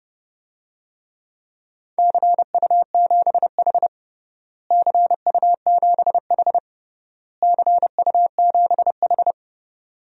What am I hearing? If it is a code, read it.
Morse code, heard as a keyed tone, about 29 wpm, 700 Hz. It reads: CU7H CU7H CU7H
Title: DXCC-100 Call Signs - 4 Characters - Repeated 3x - 30wpm